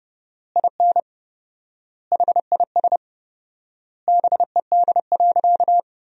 Morse code, heard as a keyed tone, about 30 wpm, 700 Hz. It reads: IN HIS BED.
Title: 4x Word-Spacing - Sentences from Top 500 Words - 30wpm